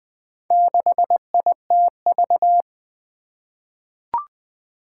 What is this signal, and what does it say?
Morse code, heard as a keyed tone, about 20 wpm, 700 Hz. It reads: 6ITV
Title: Four Challenging Letter-Number Combinations 20wpm